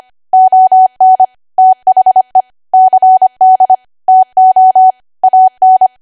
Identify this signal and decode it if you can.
Morse code, heard as a keyed tone, about 25 wpm, 750 Hz. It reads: ON THE CD TO AN
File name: cwa / W1AW-Level3 / 25WPM